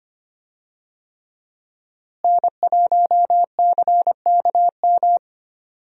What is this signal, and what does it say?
Morse code, heard as a keyed tone, about 25 wpm, 700 Hz. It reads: N1CKM
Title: USA Call Signs - 5 Characters - Repeated 3x - 25wpm